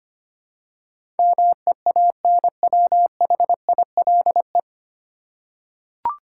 Morse code, heard as a keyed tone, about 25 wpm, 700 Hz. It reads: MEANWHILE
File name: Nine Letter Words - 25wpm